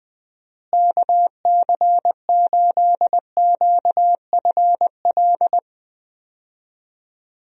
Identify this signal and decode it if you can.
Morse code, heard as a keyed tone, about 20 wpm, 700 Hz. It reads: KC8QFL
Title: USA Call Signs - 6 Characters - Repeated 3x - 20wpm